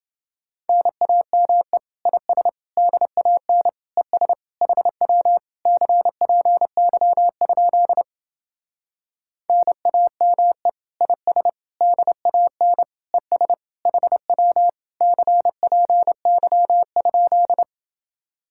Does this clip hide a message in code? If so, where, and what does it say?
Yes: Morse code from 0.7 to 17.6 s, heard as a keyed tone, about 29 wpm, 700 Hz. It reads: NAME IS DAN ES HW CPY? NAME IS DAN ES HW CPY?